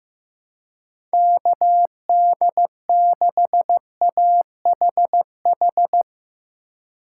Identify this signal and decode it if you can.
Morse code, heard as a keyed tone, about 15 wpm, 700 Hz. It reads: KD6AHH
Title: USA Call Signs - 6 Characters - 2x Character Spacing - 15wpm